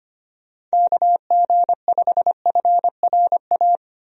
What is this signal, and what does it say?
Morse code, heard as a keyed tone, about 25 wpm, 700 Hz. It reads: KG5FRA
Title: USA Call Signs - 6 Characters - 25wpm